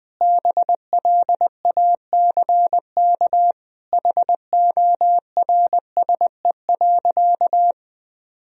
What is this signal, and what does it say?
Morse code, heard as a keyed tone, about 20 wpm, 700 Hz. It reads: BLACK HORSE.
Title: Sentences from Top 1000 Words - Review Part 2 - 20wpm